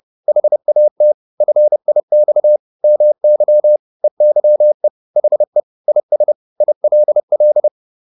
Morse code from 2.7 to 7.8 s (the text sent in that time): MY EYE HE IS ILL